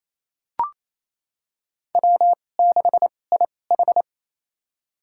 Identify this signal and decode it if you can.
Morse code, heard as a keyed tone, about 28 wpm, spaced as for 20 wpm, 700 Hz. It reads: W6IH
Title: USA Call Signs - 4 Characters - 2x Character Spacing - 28wpm